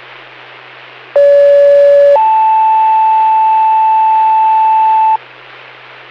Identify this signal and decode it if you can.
Two-tone page, heard as a one-second tone, then a three-second tone, about 570 then 880 Hz. It is heard from 1.2 to 5.2 s.